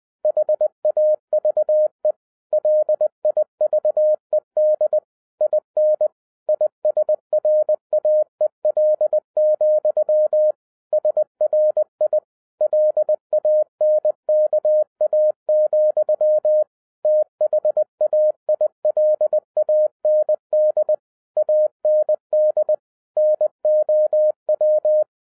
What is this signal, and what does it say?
Morse code, heard as a keyed tone, about 20 wpm, 610 Hz. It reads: HAVE LIVED IN ISRAEL, SRI LANKA, THAILAND AND NOW